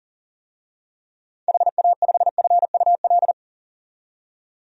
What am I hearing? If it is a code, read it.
Morse code, heard as a keyed tone, about 40 wpm, 700 Hz. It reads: HA5FUL
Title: DXCC-100 Call Signs - 6 Characters - 40wpm